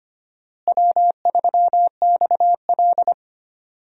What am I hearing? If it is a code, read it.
Morse code, heard as a keyed tone, about 25 wpm, 700 Hz. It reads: W3XL
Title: USA Call Signs - 4 Characters - Repeated 3x - 25wpm